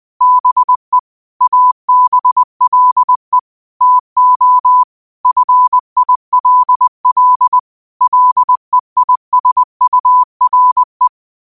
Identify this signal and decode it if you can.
Morse code, heard as a keyed tone, about 20 wpm, 1000 Hz. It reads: BE ABLE TO FILL LEISURE
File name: qotdfiles / 20wpm